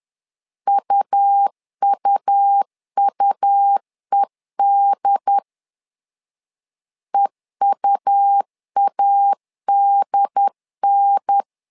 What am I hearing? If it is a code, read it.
Morse code, heard as a keyed tone, about 11 wpm, 800 Hz. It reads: UUUED EUADN
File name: LEZIONI / 3 MWRO / 3 M 10wpm 1828